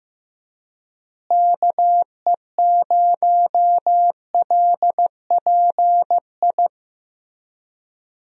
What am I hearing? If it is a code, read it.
Morse code, heard as a keyed tone, about 15 wpm, 700 Hz. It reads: KE0LPI
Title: USA Call Signs - 6 Characters - Repeated 3x - 15wpm